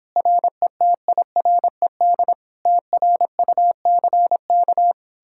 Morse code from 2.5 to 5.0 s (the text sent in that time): TRUCK